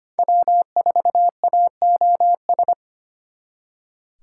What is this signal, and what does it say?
Morse code, heard as a keyed tone, about 31 wpm, 700 Hz. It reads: W4AOS
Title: Speed Racing - USA 1x3 Call Signs 25wpm